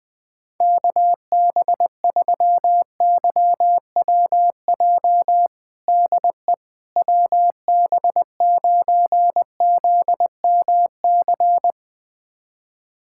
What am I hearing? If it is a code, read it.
Morse code, heard as a keyed tone, about 21 wpm, 700 Hz. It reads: KB3YWJ DE WB9ZMC
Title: USA Callsign DE Callsign 20wpm